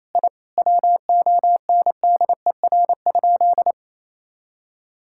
Morse code, heard as a keyed tone, about 28 wpm, 700 Hz. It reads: I WONDER?
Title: Sentences from Top 1000 Words - Review Part 5 - 28wpm